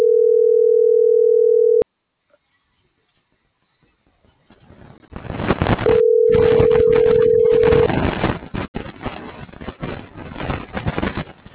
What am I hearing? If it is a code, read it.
Ringback tone (North America), 440 + 480 Hz.